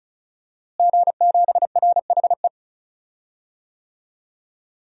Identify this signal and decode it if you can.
Morse code, heard as a keyed tone, about 36 wpm, 700 Hz. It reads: G7RHE
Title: DXCC-100 Call Signs - 5 Characters - Repeated 3x - 35wpm